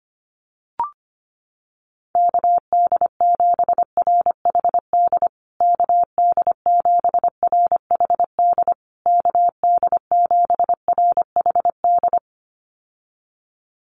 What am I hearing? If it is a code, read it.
Morse code, heard as a keyed tone, about 25 wpm, 700 Hz. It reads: KD7RHD KD7RHD KD7RHD
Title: USA Call Signs Repeated 3x 25wpm